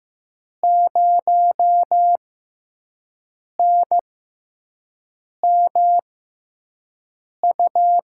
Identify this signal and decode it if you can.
Morse code, heard as a keyed tone, about 15 wpm, spaced as for 5 wpm, 700 Hz. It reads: 0NMU
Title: USA Call Signs - 6x Character Spacing - 15wpm